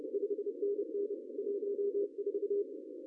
Morse code, heard as a keyed tone, about 32 wpm, 410 Hz. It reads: 5NN 14
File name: CW / cqww-cw-k5tr